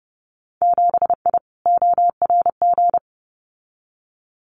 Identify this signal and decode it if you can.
Morse code, heard as a keyed tone, about 30 wpm, 700 Hz. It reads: 7I ORG
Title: Winter Field Day Exchanges Encore 30wpm